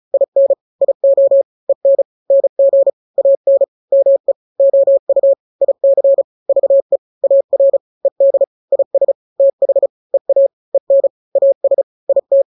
Morse code, heard as a keyed tone, about 34 wpm, 550 Hz. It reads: IN IO EN NG AN ME OU IC VE AR ED IS TH EA EN AS IT
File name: n-grams / 2-grams 35 30 WPM